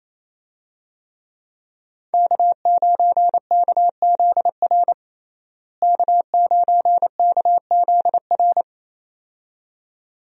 Morse code, heard as a keyed tone, about 28 wpm, 700 Hz. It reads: K9KZR K9KZR
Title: USA Call Signs - 5 Characters - Repeated 2x - 28wpm